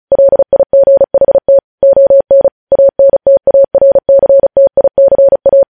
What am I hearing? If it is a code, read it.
Morse code, heard as a keyed tone, about 36 wpm, 550 Hz. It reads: LIGHT ON ANTARCTICA
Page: news headlines in morse code at 35 wpm